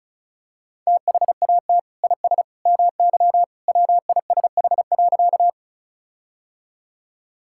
Morse code, heard as a keyed tone, about 36 wpm, 700 Hz. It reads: THAT IS MY WISH.
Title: Sentences from Top 1600 Words 35wpm